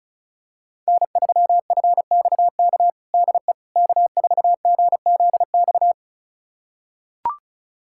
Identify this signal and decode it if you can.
Morse code, heard as a keyed tone, about 36 wpm, 690 Hz. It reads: N3FXK DE K4GZX
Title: USA Callsign DE Callsign - 5 Characters - 35wpm